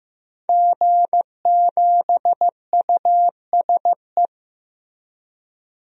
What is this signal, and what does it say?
Morse code, heard as a keyed tone, about 15 wpm, 700 Hz. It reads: G7USE
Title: DXCC-100 Call Signs - 5 Characters - 15wpm